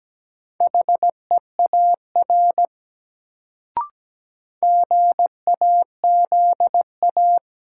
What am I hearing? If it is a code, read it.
Morse code, heard as a keyed tone, about 17 wpm, 700 Hz. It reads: HEAR GAZA
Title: Sets of 5 Words - New Word Drill - 17wpm